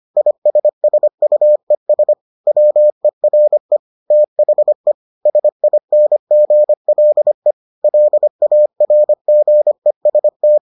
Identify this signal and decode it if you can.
Morse code, heard as a keyed tone, about 25 wpm, 600 Hz. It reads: ISSUES WERE THE SINGLE LARGEST